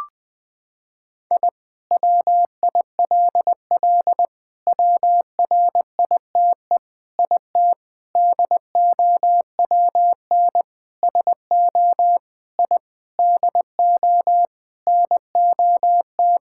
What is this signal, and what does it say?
Morse code, heard as a keyed tone, about 20 wpm, 700 Hz. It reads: I WILL WRITE IT DOWN SO I DO NOT
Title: Sentences with 4-Word Phrases - 20wpm